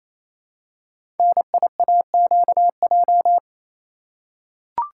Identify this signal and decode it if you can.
Morse code, heard as a keyed tone, about 28 wpm, 700 Hz. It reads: NIAQJ
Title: Code Groups - 5 Letters - 28wpm